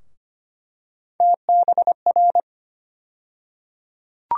Morse code, heard as a keyed tone, about 25 wpm, 700 Hz. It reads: TBR
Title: USA Call Sign Postfixes - 3 Letters 25wpm